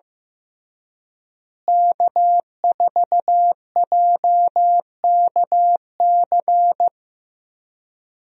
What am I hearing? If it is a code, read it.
Morse code, heard as a keyed tone, about 16 wpm, 700 Hz. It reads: K4JKC
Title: USA Call Signs - 5 Characters - Repeated 3x - 15wpm